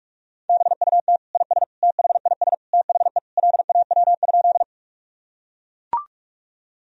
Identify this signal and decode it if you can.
Morse code, heard as a keyed tone, about 41 wpm, 700 Hz. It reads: BUT IS THIS THE LAW?
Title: Sentences from Top 1000 Words - Review Part 3 - 45wpm